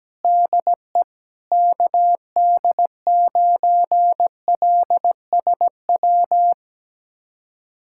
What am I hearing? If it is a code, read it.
Morse code, heard as a keyed tone, about 17 wpm, 700 Hz. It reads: DE KD9LSW
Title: USA Callsign DE Callsign 17wpm